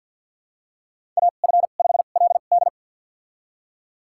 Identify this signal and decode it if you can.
Morse code, heard as a keyed tone, about 53 wpm, spaced as for 37 wpm, 690 Hz. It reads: AF5LD
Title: USA Call Signs - 2x Character Spacing - 50wpm